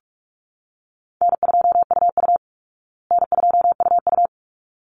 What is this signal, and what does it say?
Morse code, heard as a keyed tone, about 44 wpm, 700 Hz. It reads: N2UU N2UU
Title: Call Signs Repeated 3x - 45wpm